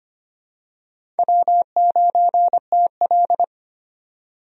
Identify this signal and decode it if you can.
Morse code, heard as a keyed tone, about 25 wpm, 700 Hz. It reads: W9TL
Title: USA Call Signs - 4 Characters - 25wpm